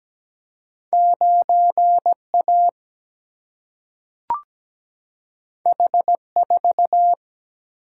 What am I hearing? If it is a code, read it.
Morse code, heard as a keyed tone, about 17 wpm, 700 Hz. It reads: 9A H4
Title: Single-Letter Single-Number 17wpm